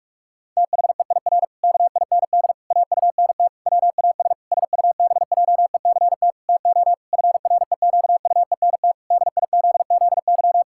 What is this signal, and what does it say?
Morse code, heard as a keyed tone, about 46 wpm, 700 Hz. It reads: THEIR KIND AUNT WAS SUBJECT TO FREQUENT DIZZY
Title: Pangram Challenge v2 - 45wpm